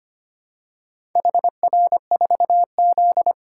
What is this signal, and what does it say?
Morse code, heard as a keyed tone, about 25 wpm, 700 Hz. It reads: HR4Z
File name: practice / 25wpm / Four Letter-Number Combinations - 25wpm